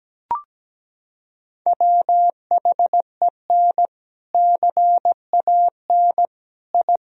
Morse code, heard as a keyed tone, about 17 wpm, 700 Hz. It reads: WHEN CAN I
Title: Sentences from Top 1000 Words - Review Part 3 - 17wpm